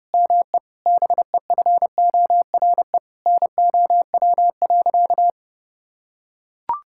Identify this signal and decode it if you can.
Morse code, heard as a keyed tone, about 32 wpm, 700 Hz. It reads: ME BEFORE NOW.
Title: Sentences from Top 1000 Words - Review Part 6 - 30wpm